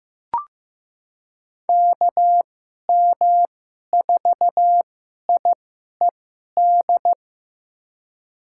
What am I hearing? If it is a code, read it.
Morse code, heard as a keyed tone, about 15 wpm, spaced as for 11 wpm, 700 Hz. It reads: KM4IED
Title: USA Call Signs - 2x Character Spacing - 15wpm